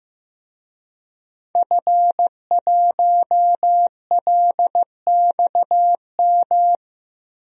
Morse code, heard as a keyed tone, about 15 wpm, 700 Hz. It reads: F1LXM
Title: DXCC-100 Call Signs - 5 Characters - Repeated 2x - 15wpm